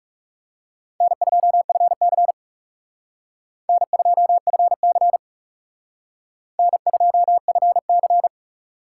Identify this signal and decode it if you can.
Morse code, heard as a keyed tone, about 36 wpm, 700 Hz. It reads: N2FC N2FC N2FC
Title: Speed Racing - USA 1x2 Call Signs 30wpm